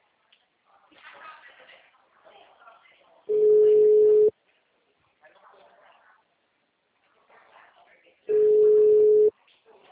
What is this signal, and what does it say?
Ringback tone (Europe, 425 Hz), with speech.